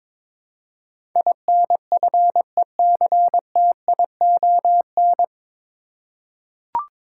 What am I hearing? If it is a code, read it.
Morse code, heard as a keyed tone, about 22 wpm, 700 Hz. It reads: INFECTION
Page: Morse Code Ninja practice 22wpm